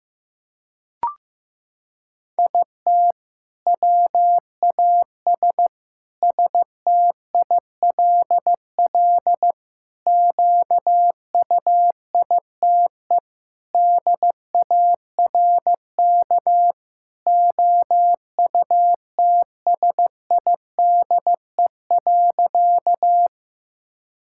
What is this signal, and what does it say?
Morse code, heard as a keyed tone, about 16 wpm, 700 Hz. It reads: IT WAS STILL QUITE DARK OUTSIDE.
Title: Sentences from Top 1000 Words - Review Part 6 - 15wpm